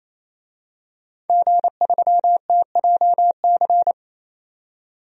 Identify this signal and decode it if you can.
Morse code, heard as a keyed tone, about 27 wpm, 700 Hz. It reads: G3TJC
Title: DXCC-100 Call Signs - 5 Characters - 28wpm